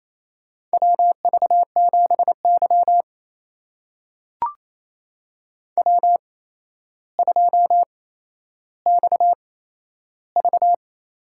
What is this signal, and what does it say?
Morse code, heard as a keyed tone, about 28 wpm, spaced as for 12 wpm, 700 Hz. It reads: WV7Y W2XV
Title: USA Call Signs - 4 Characters - 8x Character Spacing - 28wpm